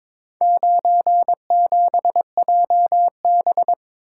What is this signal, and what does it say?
Morse code, heard as a keyed tone, about 22 wpm, 700 Hz. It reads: 97JB
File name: practice / 22wpm / Four Letter-Number Combinations - 22wpm